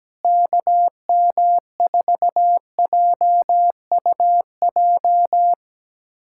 Morse code, heard as a keyed tone, about 17 wpm, 700 Hz. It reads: KM4JUJ
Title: USA Callsign DE Callsign - 6 Characters - 17wpm